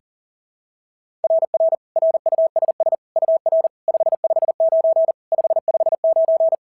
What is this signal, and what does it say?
Morse code, heard as a keyed tone, about 40 wpm, 690 Hz. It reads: RR RUSS UR 559 559
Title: SOTA QSO Practice 40wpm